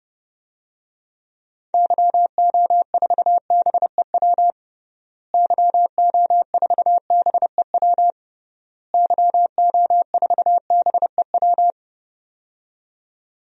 Morse code, heard as a keyed tone, about 30 wpm, 700 Hz. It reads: YO4BEW YO4BEW YO4BEW
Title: DXCC-100 Call Signs - 6 Characters - Repeated 3x - 30wpm